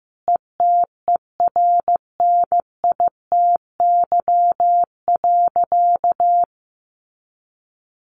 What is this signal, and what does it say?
Morse code, heard as a keyed tone, about 15 wpm, 700 Hz. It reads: ETERNITY.
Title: A Sound Of Thunder 15wpm